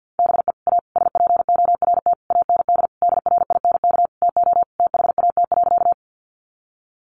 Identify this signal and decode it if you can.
Morse code, heard as a keyed tone, about 50 wpm, 700 Hz. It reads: BE A SPORT AND DRINK TO THAT.
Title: Sentences from Top 1500 Words 50wpm